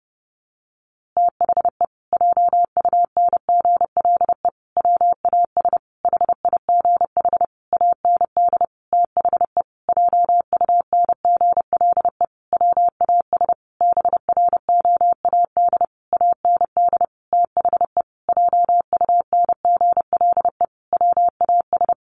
Morse code, heard as a keyed tone, about 30 wpm, 700 Hz. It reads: THE JUNGLE WAS HIGH AND THE JUNGLE WAS BROAD AND THE JUNGLE WAS
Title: A Sound Of Thunder 30wpm